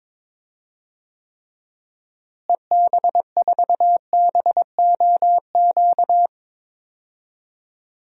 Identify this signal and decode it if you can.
Morse code, heard as a keyed tone, about 22 wpm, 700 Hz. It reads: EB4BOQ
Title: DXCC-100 Call Signs - 6 Characters - Repeated 3x - 22wpm